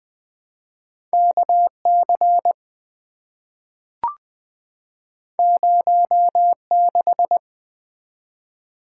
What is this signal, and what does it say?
Morse code, heard as a keyed tone, about 20 wpm, 700 Hz. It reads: KC 06
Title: Two Letter-Number Combinations - 20wpm